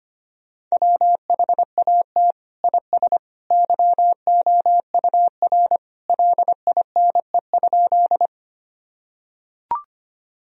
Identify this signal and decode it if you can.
Morse code, heard as a keyed tone, about 25 wpm, 700 Hz. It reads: WHAT IS YOUR LINE?
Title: Sentences from Top 1000 Words - Review Part 1 - 25wpm